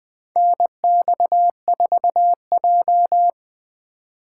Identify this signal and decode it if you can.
Morse code, heard as a keyed tone, about 21 wpm, 700 Hz. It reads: NX4J
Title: USA Call Signs - 4x Character Spacing - 20wpm